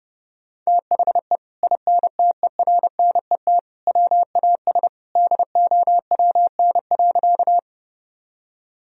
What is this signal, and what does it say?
Morse code, heard as a keyed tone, about 30 wpm, 700 Hz. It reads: THE INTERNET WAS DOWN.
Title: Sentences from Top 1000 Words - Review Part 4 - 30wpm